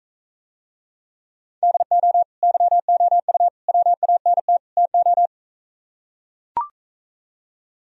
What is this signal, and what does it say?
Morse code, heard as a keyed tone, about 44 wpm, 700 Hz. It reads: DO YOU WANT TO